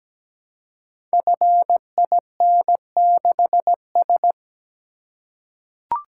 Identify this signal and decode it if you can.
Morse code, heard as a keyed tone, about 17 wpm, 700 Hz. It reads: FIN6S
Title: Code Groups - 5 Letter:Numbers - 17wpm